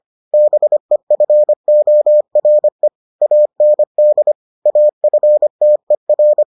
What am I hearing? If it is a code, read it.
Morse code, heard as a keyed tone, about 25 wpm, 600 Hz. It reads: BEFORE AND AFTER